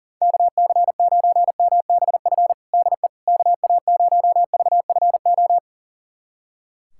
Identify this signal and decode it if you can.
Morse code, heard as a keyed tone, about 40 wpm, 700 Hz. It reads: KC9MBF DE KA0VFO
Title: USA Callsign DE Callsign - 6 Characters - 40wpm